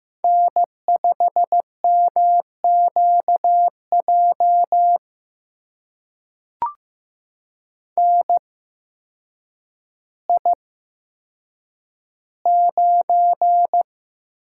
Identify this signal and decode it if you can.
Morse code, heard as a keyed tone, about 15 wpm, spaced as for 8 wpm, 700 Hz. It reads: N5MQJ NI9
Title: USA Call Signs - 5 Characters - 8x Character Spacing - 15wpm